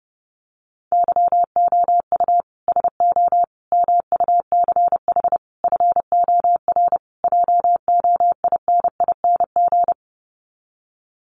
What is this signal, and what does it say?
Morse code, heard as a keyed tone, about 30 wpm, 700 Hz. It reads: YOU SO MUCH FOR JOINING
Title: Sets of 5 Words Encore2 30wpm